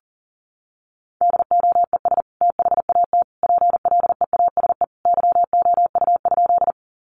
Morse code, heard as a keyed tone, about 37 wpm, 700 Hz. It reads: DOES THAT PLEASE YOU?
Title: Sentences from Top 1500 Words 40wpm